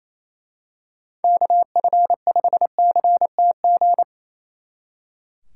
Morse code, heard as a keyed tone, about 28 wpm, 700 Hz. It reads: KF5CTG
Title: Speed Racing - USA 2x3 Call Signs 28wpm